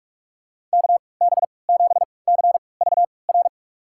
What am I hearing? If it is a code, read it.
Morse code, heard as a keyed tone, about 46 wpm, spaced as for 25 wpm, 700 Hz. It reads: KB7CVR
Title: USA Call Signs - 6 Characters - 3x Character Spacing - 45wpm